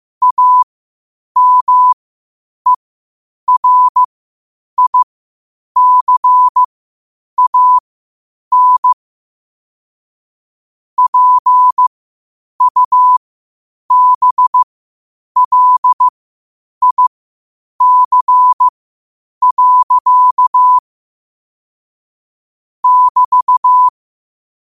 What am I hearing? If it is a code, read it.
Morse code, heard as a keyed tone, about 15 wpm, spaced as for 8 wpm, 1000 Hz. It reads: AMERICAN PUBLIC. <BT>
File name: qotdfiles / 8wpm